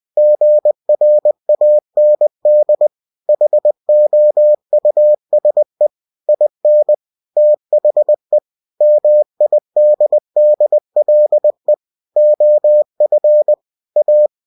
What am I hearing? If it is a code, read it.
Morse code, heard as a keyed tone, about 20 wpm, 600 Hz. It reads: GRAND HOUSE IN THE MIDDLE OF A
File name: cw-books / winnie-the-pooh / winnie-20wpm-page-0020